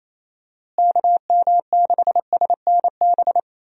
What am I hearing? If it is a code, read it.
Morse code, heard as a keyed tone, about 28 wpm, 700 Hz. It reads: KM6SNB
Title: USA Call Signs - 6 Characters - Repeated 2x - 28wpm